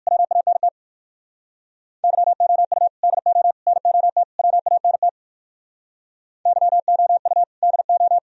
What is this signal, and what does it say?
Morse code, heard as a keyed tone, about 58 wpm, 700 Hz. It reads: WANT YOU DO NOT WANT YOU DO